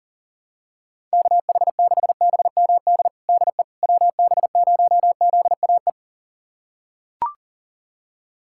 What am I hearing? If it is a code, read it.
Morse code, heard as a keyed tone, about 40 wpm, 700 Hz. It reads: KH6BMD DE WB0ZAE
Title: USA Callsign DE Callsign - 6 Characters - 40wpm